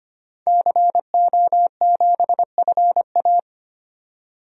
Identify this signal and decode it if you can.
Morse code, heard as a keyed tone, about 25 wpm, 700 Hz. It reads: CO7FA
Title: DXCC-100 Call Signs - 5 Characters - 25wpm